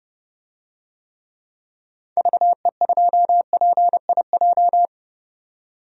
Morse code, heard as a keyed tone, about 30 wpm, 700 Hz. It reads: VE2PIJ
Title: DXCC-100 Call Signs - 6 Characters - 30wpm